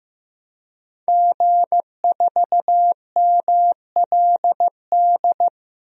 Morse code, heard as a keyed tone, about 15 wpm, 700 Hz. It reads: G4MLD
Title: DXCC-100 Call Signs - 5 Characters - Repeated 3x - 15wpm